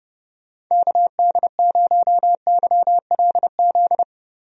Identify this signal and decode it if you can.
Morse code, heard as a keyed tone, about 30 wpm, 700 Hz. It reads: KD0YLZ
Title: USA Call Signs - 6 Characters - Repeated 2x - 30wpm